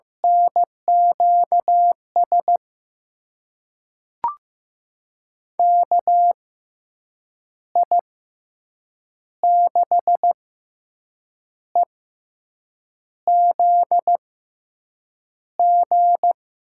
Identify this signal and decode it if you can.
Morse code, heard as a keyed tone, about 15 wpm, spaced as for 6 wpm, 700 Hz. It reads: NQS KI6EZG
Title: USA Call Signs - 6 Characters - 6x Character Spacing - 15wpm